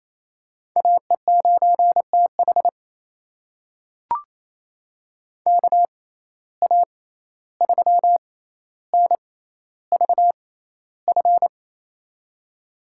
Morse code, heard as a keyed tone, about 28 wpm, spaced as for 14 wpm, 700 Hz. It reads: AE9TH KA3NVF
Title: USA Call Signs - 6x Character Spacing - 28wpm